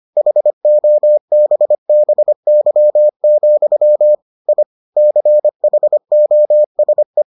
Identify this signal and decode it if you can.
Morse code, heard as a keyed tone, about 25 wpm, 600 Hz. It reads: HOBBY, I CHOSE